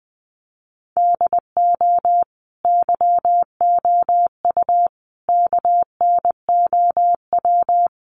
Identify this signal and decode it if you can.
Morse code, heard as a keyed tone, about 20 wpm, 700 Hz. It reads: DO YOU KNOW
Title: Sentences from Top 1200 Words 20wpm